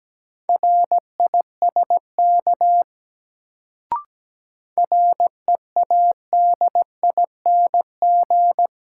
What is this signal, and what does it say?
Morse code, heard as a keyed tone, about 17 wpm, 700 Hz. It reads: RISK READING
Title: Sentences from Top 600 Words - New Word Drill - 17wpm